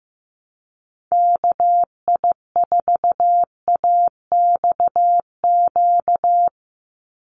Morse code, heard as a keyed tone, about 15 wpm, 700 Hz. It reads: KI4AXQ
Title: USA Call Signs 15wpm